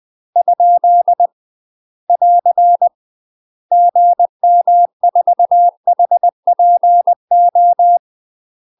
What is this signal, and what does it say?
Morse code, heard as a keyed tone, about 20 wpm, 700 Hz. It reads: ? <AR> GM4HPO
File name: UK MASTER QSOs TRAINING / 20WPM / UK MASTER QSOs 0013